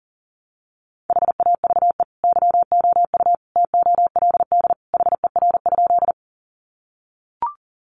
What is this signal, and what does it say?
Morse code, heard as a keyed tone, about 40 wpm, 700 Hz. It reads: HAVE YOU TOLD HER?
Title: Sentences from Top 400 Words 40wpm